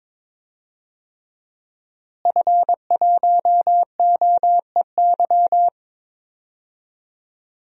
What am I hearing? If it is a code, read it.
Morse code, heard as a keyed tone, about 22 wpm, 700 Hz. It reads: F1OEY
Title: DXCC-100 Call Signs - 5 Characters - Repeated 3x - 22wpm